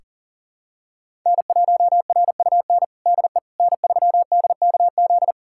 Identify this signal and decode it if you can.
Morse code, heard as a keyed tone, about 40 wpm, 700 Hz. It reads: N1RUN DE N3DKZ
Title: USA Callsign DE Callsign - 5 Characters - 40wpm